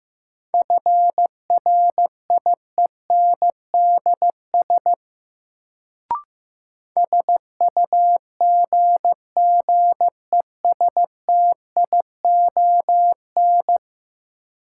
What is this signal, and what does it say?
Morse code, heard as a keyed tone, about 15 wpm, 700 Hz. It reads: FRIENDS SUGGESTION
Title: Sentences from Top 1600 Words - New Word Drill Alt - 15wpm